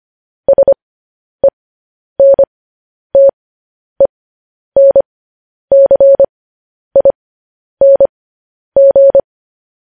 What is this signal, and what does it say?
Morse code, heard as a keyed tone, about 25 wpm, spaced as for 10 wpm, 550 Hz. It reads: SENTENCING
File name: podcast / News headlines-1x-25x10